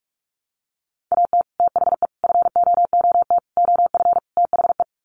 Morse code, heard as a keyed tone, about 44 wpm, 700 Hz. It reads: AT THE FOOT OF THE
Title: Sets of 5 Words 45wpm